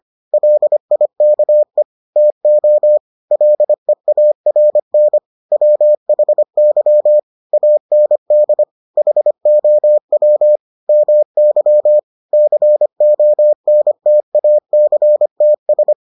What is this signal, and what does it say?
Morse code, heard as a keyed tone, about 25 wpm, 600 Hz. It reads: LIKE TO LEARN WHY AND HOW MY CONTACTS